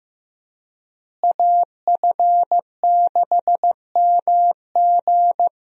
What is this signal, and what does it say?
Morse code, heard as a keyed tone, about 15 wpm, 700 Hz. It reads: AF6MG
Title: USA Call Signs - 5 Characters - Repeated 3x - 15wpm